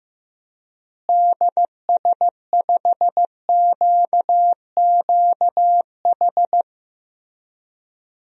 Morse code, heard as a keyed tone, about 15 wpm, 700 Hz. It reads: DS5QQH
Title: DXCC-100 Call Signs - 6 Characters - Repeated 3x - 15wpm